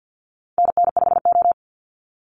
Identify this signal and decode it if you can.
Morse code, heard as a keyed tone, about 48 wpm, 700 Hz. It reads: NN5O
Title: Call Signs Repeated 2x - 50wpm